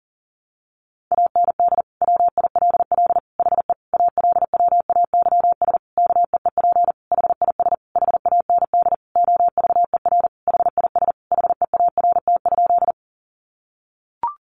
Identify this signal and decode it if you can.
Morse code, heard as a keyed tone, about 40 wpm, 700 Hz. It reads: AND WILL HE ALWAYS KEEP HIS HAND OVER HIS HEART?
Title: Sentences from Top 1200 Words 40wpm